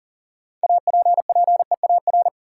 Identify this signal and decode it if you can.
Morse code, heard as a keyed tone, about 40 wpm, 700 Hz. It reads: APPEAR
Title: Sentences from Top 1200 Words - New Word Drill Alt - 40wpm